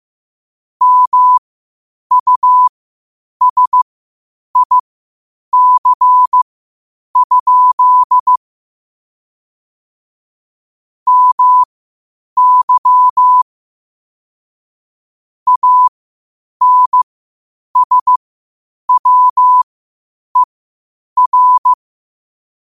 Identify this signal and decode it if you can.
Morse code, heard as a keyed tone, about 15 wpm, spaced as for 8 wpm, 1000 Hz. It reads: MUSIC? MY ANSWER
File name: qotdfiles / 8wpm